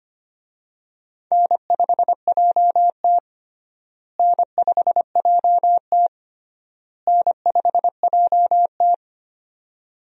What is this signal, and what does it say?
Morse code, heard as a keyed tone, about 25 wpm, 700 Hz. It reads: N5JT N5JT N5JT
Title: USA Call Signs - 4 Characters - Repeated 3x - 25wpm